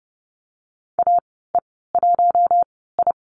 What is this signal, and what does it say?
Morse code, heard as a keyed tone, about 30 wpm, spaced as for 17 wpm, 700 Hz. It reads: AE1I
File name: Lesson 14g - Number 2 - Call Signs 30wpm